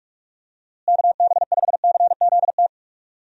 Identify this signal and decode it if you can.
Morse code, heard as a keyed tone, about 46 wpm, 700 Hz. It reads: KB5CZT
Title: USA Call Signs - 6 Characters - Repeated 3x - 45wpm